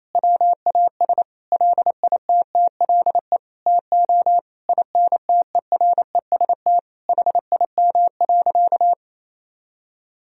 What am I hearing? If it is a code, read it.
Morse code, heard as a keyed tone, about 27 wpm, 700 Hz. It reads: WAS LITTLE TO INTEREST HIM.
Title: Sentences from Top 1000 Words - Review Part 2 - 28wpm